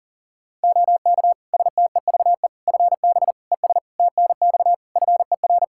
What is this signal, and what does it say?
Morse code, heard as a keyed tone, about 40 wpm, 700 Hz. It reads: OK STEVE FB ES TNX FER
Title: Practice QSO - Exchange 2 - Info - 40wpm